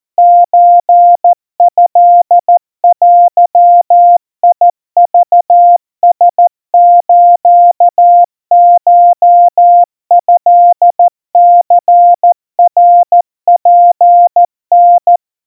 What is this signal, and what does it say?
Morse code (Japanese Wabun code), heard as a keyed tone, about 14 wpm, 700 Hz. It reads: ソトデクラスコトニナツタ
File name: kawa / Morse sample / kamyu 50 cpm 700 Hz Crispy